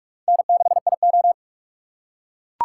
Morse code, heard as a keyed tone, about 40 wpm, 700 Hz. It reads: N6IO